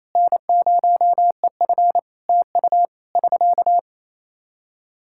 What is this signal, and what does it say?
Morse code, heard as a keyed tone, about 28 wpm, 700 Hz. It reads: N0EF TU <SK>
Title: Practice QSO - Exchange 4 - Ending - 28wpm